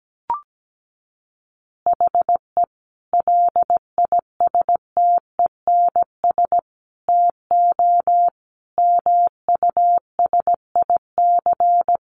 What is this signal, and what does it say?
Morse code, heard as a keyed tone, about 17 wpm, 700 Hz. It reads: HE LISTENS TO MUSIC
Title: Additional Easy Phrases 17wpm